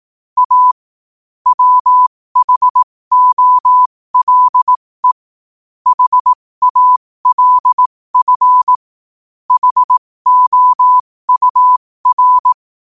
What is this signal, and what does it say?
Morse code, heard as a keyed tone, about 18 wpm, spaced as for 15 wpm, 1000 Hz. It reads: A WHOLE HALF HOUR
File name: 15wpm